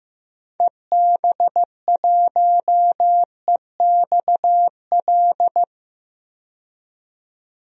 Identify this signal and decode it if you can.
Morse code, heard as a keyed tone, about 15 wpm, 700 Hz. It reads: EB1EXL
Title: DXCC-100 Call Signs - 6 Characters - 15wpm